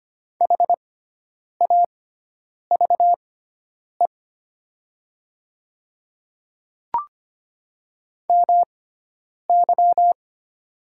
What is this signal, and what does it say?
Morse code, heard as a keyed tone, about 25 wpm, spaced as for 9 wpm, 700 Hz. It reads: HAVE MY
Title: 6x Character Spacing - Top 100 Words - 25wpm